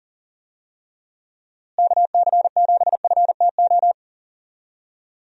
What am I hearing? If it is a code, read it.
Morse code, heard as a keyed tone, about 40 wpm, 700 Hz. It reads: KC7FTO
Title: USA Call Signs - 6 Characters - Repeated 2x - 40wpm